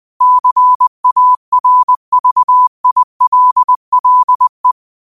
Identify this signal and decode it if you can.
Morse code, heard as a keyed tone, about 20 wpm, 1000 Hz. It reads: CARVILLE
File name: qotdfiles / 20wpm